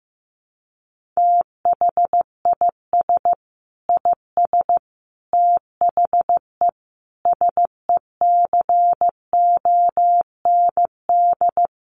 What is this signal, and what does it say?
Morse code, heard as a keyed tone, about 14 wpm, 700 Hz. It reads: THIS IS THE SECOND
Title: Sentences from Top 1400 Words 15wpm